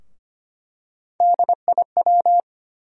Morse code, heard as a keyed tone, about 25 wpm, 700 Hz. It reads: DIW
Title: USA Call Sign Postfixes - 3 Letters 25wpm